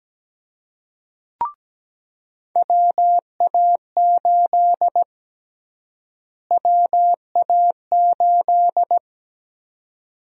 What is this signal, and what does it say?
Morse code, heard as a keyed tone, about 17 wpm, 700 Hz. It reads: WA8 WA8
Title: USA Call Sign Prefixes - Repeated 2x - 17wpm